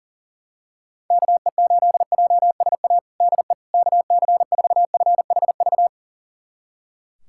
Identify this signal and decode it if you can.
Morse code, heard as a keyed tone, about 40 wpm, 700 Hz. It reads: KE8JSA DE KC4FHV
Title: USA Callsign DE Callsign - 6 Characters - 40wpm